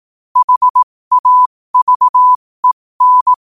Morse code, heard as a keyed tone, about 18 wpm, spaced as for 15 wpm, 1000 Hz. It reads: HAVEN
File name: qotdfiles / 15wpm